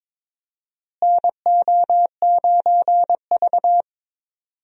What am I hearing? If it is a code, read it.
Morse code, heard as a keyed tone, about 21 wpm, 700 Hz. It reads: NO9V
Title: USA Call Signs - 4 Characters - Repeated 2x - 22wpm